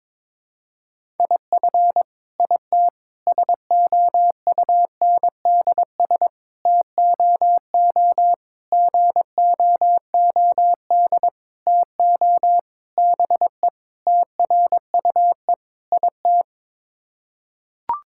Morse code, heard as a keyed tone, about 22 wpm, 700 Hz. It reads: IF IT SOUNDS TOO GOOD TO BE TRUE IT
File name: Sets of 9 Words - 22wpm